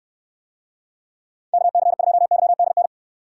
Morse code, heard as a keyed tone, about 69 wpm, 700 Hz. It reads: KC2YDN